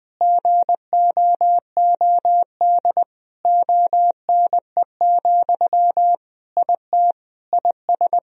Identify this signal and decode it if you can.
Morse code, heard as a keyed tone, about 20 wpm, 700 Hz. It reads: GOOD ONE, IT IS
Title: Sentences from Top 1000 Words - Review Part 1 - 20wpm